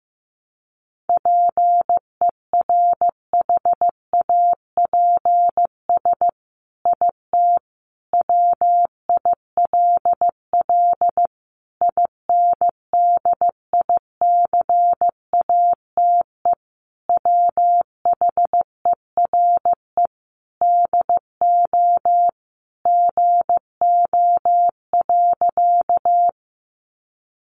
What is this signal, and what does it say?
Morse code, heard as a keyed tone, about 15 wpm, 700 Hz. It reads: PERHAPS IT WILL INDICATE WHERE DO GO.
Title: Sentences from Top 1300 Words 15wpm